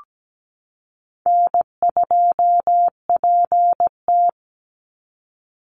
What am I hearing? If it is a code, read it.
Morse code, heard as a keyed tone, about 17 wpm, 700 Hz. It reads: N2PT
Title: USA Call Signs 17wpm